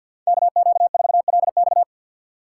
Morse code, heard as a keyed tone, about 50 wpm, 700 Hz. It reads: KQ4LX
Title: USA Call Signs - 5 Characters - 6x Character Spacing - 50wpm